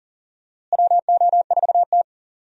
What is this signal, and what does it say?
Morse code, heard as a keyed tone, about 40 wpm, 700 Hz. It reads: WO4T
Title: USA Call Signs - 4 Characters - Repeated 3x - 40wpm